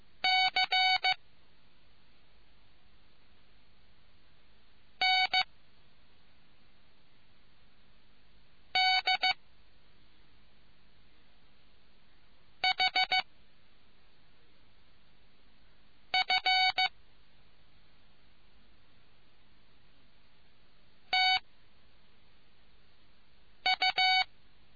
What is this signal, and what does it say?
Morse code, heard as a keyed tone, about 15 wpm, 2250 Hz. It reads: C N D H F T U